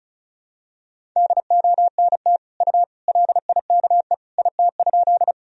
Morse code, heard as a keyed tone, about 34 wpm, 700 Hz. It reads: DONT U LIKE IT?
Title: Sentences from Top 300 Words and Abbreviations 35wpm